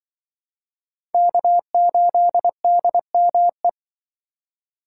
Morse code, heard as a keyed tone, about 25 wpm, 700 Hz. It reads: K8DME